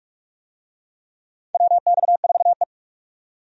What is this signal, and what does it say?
Morse code, heard as a keyed tone, about 46 wpm, 700 Hz. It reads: WX4E
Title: USA Call Signs - 4 Characters - 45wpm